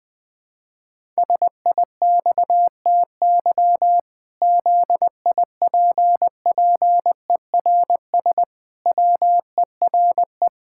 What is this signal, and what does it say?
Morse code, heard as a keyed tone, about 21 wpm, 700 Hz. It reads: SIXTY ZIPPERS WERE
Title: Pangram Challenge v2 - 20wpm